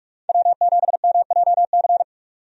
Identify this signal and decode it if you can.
Morse code, heard as a keyed tone, about 45 wpm, 700 Hz. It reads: W7MJC
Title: USA Call Signs - 5 Characters - 45wpm